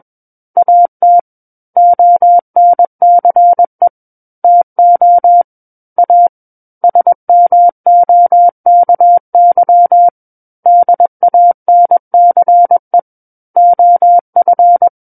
Morse code, heard as a keyed tone, about 21 wpm, 700 Hz. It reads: AT ONCE TO A SMOKY DANCE OF
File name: Book One / 12-Chapter 12 - 21 WPM